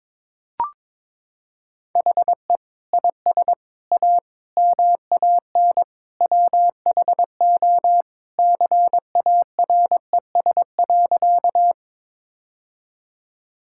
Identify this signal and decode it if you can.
Morse code, heard as a keyed tone, about 22 wpm, 700 Hz. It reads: HE IS A MAN WHO CARES.
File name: practice / 22wpm / Sentences with 3-Word Phrases - 22wpm